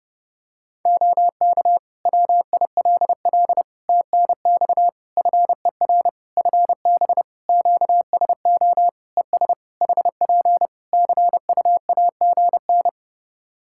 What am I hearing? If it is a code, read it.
Morse code, heard as a keyed tone, about 30 wpm, 700 Hz. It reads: OK WILL TNX FER FB QSO ES HP CUAGN
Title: Practice QSO - Exchange 4 - Ending - 30wpm